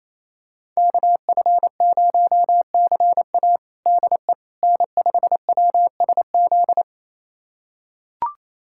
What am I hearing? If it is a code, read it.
Morse code, heard as a keyed tone, about 27 wpm, 700 Hz. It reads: KF0CA DE N5WSZ
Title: USA Callsign DE Callsign - 5 Characters - 28wpm